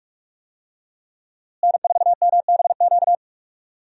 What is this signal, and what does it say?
Morse code, heard as a keyed tone, about 46 wpm, 700 Hz. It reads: N4MBQ
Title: USA Call Signs - 5 Characters - Repeated 3x - 45wpm